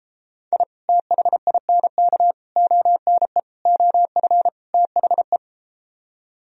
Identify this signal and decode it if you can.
Morse code, heard as a keyed tone, about 33 wpm, 700 Hz. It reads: I THINK ONE OF THE